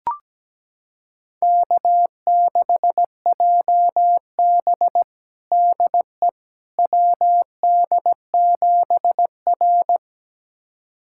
Morse code, heard as a keyed tone, about 17 wpm, 700 Hz. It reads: K6JB DE WD7R
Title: USA Callsign DE Callsign - 4 Characters - 17wpm